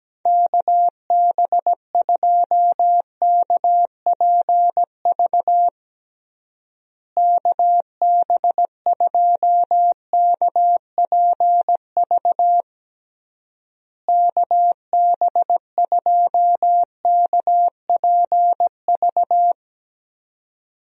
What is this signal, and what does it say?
Morse code, heard as a keyed tone, about 17 wpm, 700 Hz. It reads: KB2KPV KB2KPV KB2KPV
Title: USA Call Signs - 6 Characters - Repeated 3x - 17wpm